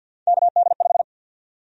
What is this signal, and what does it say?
Morse code, heard as a keyed tone, about 50 wpm, 700 Hz. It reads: KD5
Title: USA Call Sign Prefixes - 50wpm